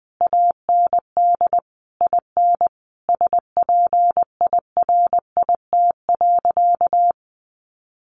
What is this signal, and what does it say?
Morse code, heard as a keyed tone, about 20 wpm, 700 Hz. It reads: AND IN SPIRIT.
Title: Sentences from Top 1400 Words 20wpm